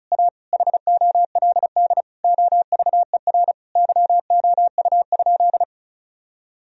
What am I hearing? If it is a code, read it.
Morse code, heard as a keyed tone, about 34 wpm, 700 Hz. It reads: A HOLD OVER YOU?
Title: Sentences from Top 1000 Words - Review Part 2 - 35wpm